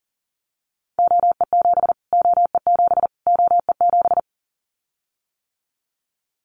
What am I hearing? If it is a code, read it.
Morse code, heard as a keyed tone, about 40 wpm, 700 Hz. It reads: OE7 OE7 OE7
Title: Callsign Prefixes for Top 25 DXCC - Repeated 3x - 40wpm